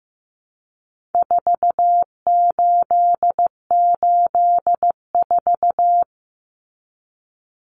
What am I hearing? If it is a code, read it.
Morse code, heard as a keyed tone, about 14 wpm, 700 Hz. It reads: 4884
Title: Common Cut Numbers - 4-Digit 15wpm